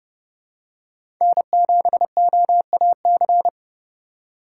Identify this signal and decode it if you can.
Morse code, heard as a keyed tone, about 31 wpm, 700 Hz. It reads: N7OAC
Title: USA Call Signs - 5 Characters - 8x Character Spacing - 30wpm